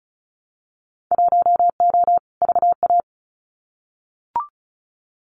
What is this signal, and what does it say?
Morse code, heard as a keyed tone, about 34 wpm, 700 Hz. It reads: 1O VA
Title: Winter Field Day Exchanges 35wpm